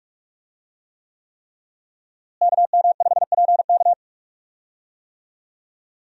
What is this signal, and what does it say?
Morse code, heard as a keyed tone, about 44 wpm, 700 Hz. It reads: KM5PK
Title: USA Call Signs - 5 Characters - Repeated 3x - 45wpm